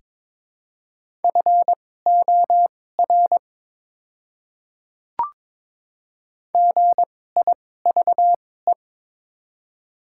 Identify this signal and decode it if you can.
Morse code, heard as a keyed tone, about 22 wpm, spaced as for 16 wpm, 700 Hz. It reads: FOR GIVE
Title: 2x Character Spacing - Top 300 Words - 22wpm